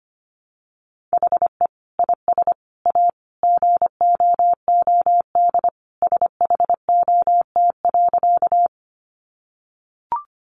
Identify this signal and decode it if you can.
Morse code, heard as a keyed tone, about 25 wpm, 700 Hz. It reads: HE IS A GOOD SHOT.
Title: Sentences from Top 1100 Words 25wpm